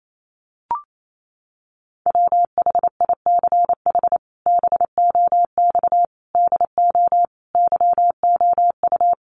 Morse code, heard as a keyed tone, about 28 wpm, 700 Hz. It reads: WHICH BOX DO YOU
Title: Sentences from Top 500 Words 28wpm